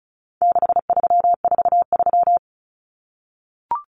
Morse code, heard as a keyed tone, about 34 wpm, 700 Hz. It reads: 6343
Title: Common Cut Numbers - 4-Digit 35wpm